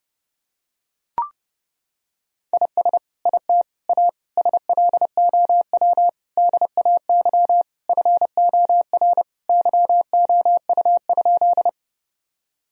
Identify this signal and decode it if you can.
Morse code, heard as a keyed tone, about 30 wpm, 700 Hz. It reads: IS IT A SLOW DAY FOR YOU?
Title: Sentences from Top 1000 Words - Review Part 2 - 30wpm